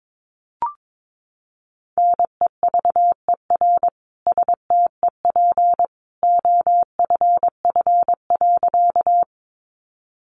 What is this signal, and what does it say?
Morse code, heard as a keyed tone, about 22 wpm, 700 Hz. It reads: NEVER STEP OFF.
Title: A Sound Of Thunder 22wpm